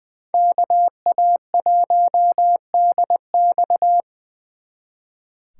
Morse code, heard as a keyed tone, about 25 wpm, 700 Hz. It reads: KA1DX
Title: Speed Racing - USA 2x2 Call Signs 20wpm